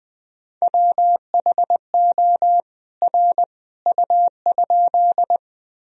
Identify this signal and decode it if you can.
Morse code, heard as a keyed tone, about 21 wpm, 700 Hz. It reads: WHO R U?
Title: Sentences from Top 100 Words and Abbreviations 20wpm